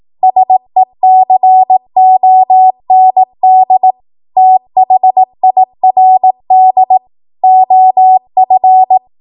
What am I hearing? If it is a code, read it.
Morse code, heard as a keyed tone, about 18 wpm, 750 Hz. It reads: SECOND THIRD OF